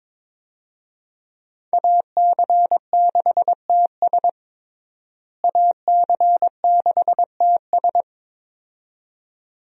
Morse code, heard as a keyed tone, about 22 wpm, 700 Hz. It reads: AC6TS AC6TS
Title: USA Call Signs - 5 Characters - Repeated 2x - 22wpm